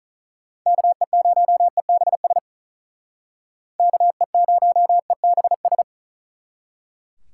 Morse code, heard as a keyed tone, about 45 wpm, 700 Hz. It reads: KE0EBS KE0EBS